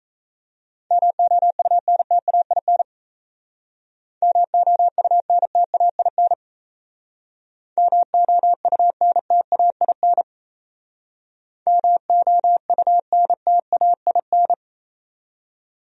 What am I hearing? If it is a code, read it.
Morse code, heard as a keyed tone, about 36 wpm, 700 Hz. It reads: MOUNTAIN MOUNTAIN MOUNTAIN MOUNTAIN